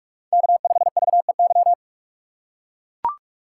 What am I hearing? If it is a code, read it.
Morse code, heard as a keyed tone, about 44 wpm, 690 Hz. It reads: K5VEY
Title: USA Call Signs - 5 Characters - 6x Character Spacing - 45wpm